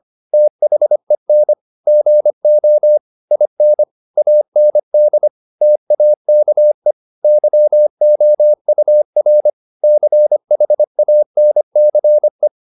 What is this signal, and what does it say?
Morse code, heard as a keyed tone, about 25 wpm, 600 Hz. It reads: THEN GO IN AND TAKE YOUR CHANCE
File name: the old man and the sea part 1 25wpm